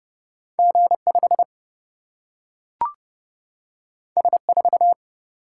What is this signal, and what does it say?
Morse code, heard as a keyed tone, about 30 wpm, 700 Hz. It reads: G5 S4
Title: Single-Letter Single-Number 30wpm